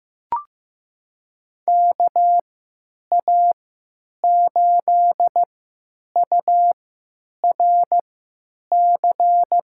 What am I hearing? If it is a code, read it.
Morse code, heard as a keyed tone, about 15 wpm, spaced as for 8 wpm, 700 Hz. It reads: KA8URC
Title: USA Call Signs - 6 Characters - 3x Character Spacing - 15wpm